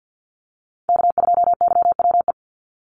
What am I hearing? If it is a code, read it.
Morse code, heard as a keyed tone, about 50 wpm, 700 Hz. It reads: K2YWE
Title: Call Signs - 50wpm